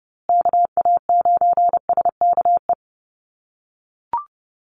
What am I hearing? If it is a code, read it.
Morse code, heard as a keyed tone, about 30 wpm, 700 Hz. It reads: KA9SKE
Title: USA Call Signs Repeated 2x 30wpm